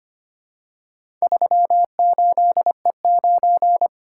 Morse code, heard as a keyed tone, about 25 wpm, 700 Hz. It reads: 38E9
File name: practice / 25wpm / Four Letter-Number Combinations - 25wpm